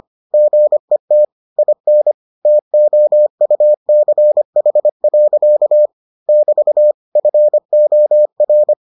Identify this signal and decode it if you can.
Morse code, heard as a keyed tone, about 25 wpm, 600 Hz. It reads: GET IN TOUCH. <BT> FOR